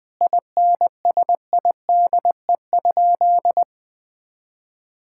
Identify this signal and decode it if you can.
Morse code, heard as a keyed tone, about 20 wpm, 700 Hz. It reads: INSIDE?
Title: Sentences from Top 1000 Words - Review Part 6 - 20wpm